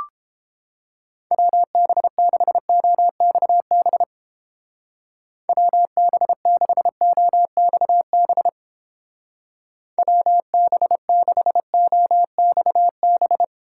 Morse code, heard as a keyed tone, about 32 wpm, 700 Hz. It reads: WB6OXB WB6OXB WB6OXB